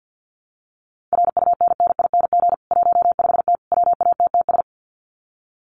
Morse code, heard as a keyed tone, about 50 wpm, 700 Hz. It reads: RUNNING 15T WATTS
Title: Common QSO Phrases 50wpm